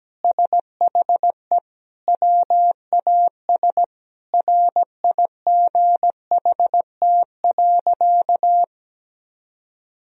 Morse code, heard as a keyed tone, about 17 wpm, 700 Hz. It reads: SHE WAS RIGHT.
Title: Sentences from Top 1000 Words - Review Part 1 - 17wpm